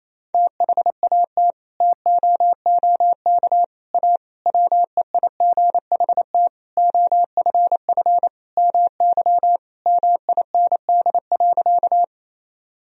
Morse code, heard as a keyed tone, about 28 wpm, 700 Hz. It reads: THAT TOOK A WEIGHT OFF MY MIND.
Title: Sentences from Top 1000 Words - Review Part 3 - 28wpm